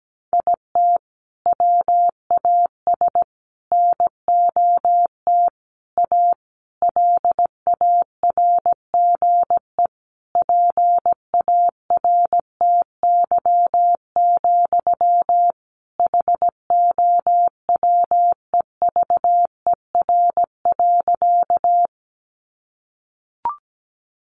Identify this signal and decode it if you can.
Morse code, heard as a keyed tone, about 17 wpm, 700 Hz. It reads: IT WAS NOT A LARGE PARTY, HOWEVER.
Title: Sentences from Top 1400 Words 17wpm